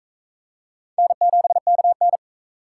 Morse code, heard as a keyed tone, about 40 wpm, 700 Hz. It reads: N7KN